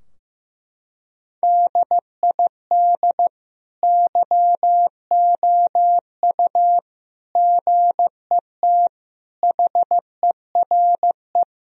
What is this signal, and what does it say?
Morse code, heard as a keyed tone, about 15 wpm, 700 Hz. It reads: DID YOU GET HERE
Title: Sentences from Top 1000 Words - Review Part 3 - 15wpm